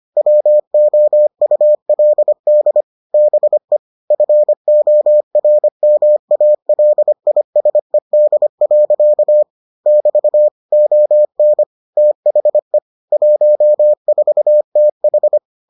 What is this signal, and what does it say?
Morse code, heard as a keyed tone, about 25 wpm, 600 Hz. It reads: WOULD BE FORMALISED. <BT> ON THE 14TH